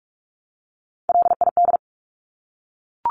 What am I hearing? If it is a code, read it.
Morse code, heard as a keyed tone, about 44 wpm, 700 Hz. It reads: LID
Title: Abbreviations 45wpm